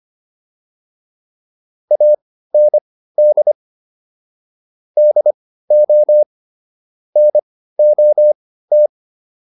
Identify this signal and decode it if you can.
Morse code, heard as a keyed tone, about 25 wpm, spaced as for 15 wpm, 600 Hz. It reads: AND DO NOT
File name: WD404 15